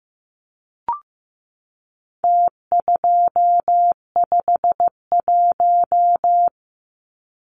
Morse code, heard as a keyed tone, about 15 wpm, 700 Hz. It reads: T251
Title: Numbers - Contest Style - 4-Digit 15wpm